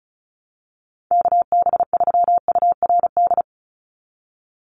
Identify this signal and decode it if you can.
Morse code, heard as a keyed tone, about 34 wpm, 700 Hz. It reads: KB3URD
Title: USA Call Signs 35wpm